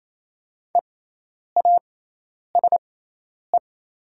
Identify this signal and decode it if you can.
Morse code, heard as a keyed tone, about 28 wpm, spaced as for 10 wpm, 700 Hz. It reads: EASE
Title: 6x Character Spacing - Top 500 Words - 28wpm